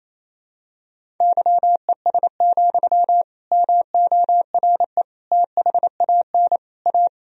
Morse code, heard as a keyed tone, about 28 wpm, 700 Hz. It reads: YES, MORE THAN A
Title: Sentences from Top 1000 Words - Review Part 1 - 28wpm